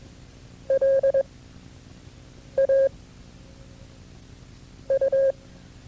Morse code, heard as a keyed tone, about 22 wpm, spaced as for 5 wpm, 540 Hz. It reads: LAU
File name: FreeFiles / CW-practice / KB6MT-tape2-side1